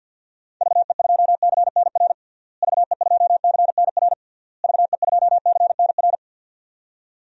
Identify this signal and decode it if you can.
Morse code, heard as a keyed tone, about 51 wpm, 700 Hz. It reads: VE2CNR VE2CNR VE2CNR
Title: DXCC-100 Call Signs - 6 Characters - Repeated 3x - 50wpm